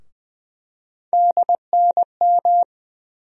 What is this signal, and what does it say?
Morse code, heard as a keyed tone, about 20 wpm, 700 Hz. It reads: DNM
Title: USA Call Sign Postfixes - 3 Letters 20wpm